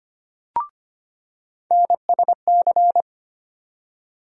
Morse code, heard as a keyed tone, about 25 wpm, 700 Hz. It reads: NSC
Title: USA Call Sign Postfixes - 3 Letters 25wpm